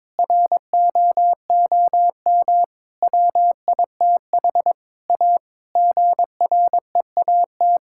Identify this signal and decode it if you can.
Morse code, heard as a keyed tone, about 22 wpm, 700 Hz. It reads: ROOM WITH A GREAT
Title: Sentences from Top 1600 Words 22wpm